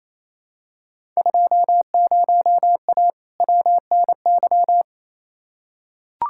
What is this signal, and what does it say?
Morse code, heard as a keyed tone, about 28 wpm, 700 Hz. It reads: 20A WNY
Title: Field Day Exchanges Encore - 28wpm